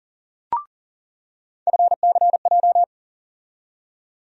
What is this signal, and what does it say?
Morse code, heard as a keyed tone, about 37 wpm, 700 Hz. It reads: FCJ
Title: USA Call Sign Postfixes - 3 Letters 40wpm